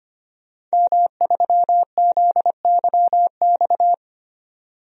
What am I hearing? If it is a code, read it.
Morse code, heard as a keyed tone, about 25 wpm, 700 Hz. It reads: M3ZYX
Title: DXCC-100 Call Signs - 5 Characters - 25wpm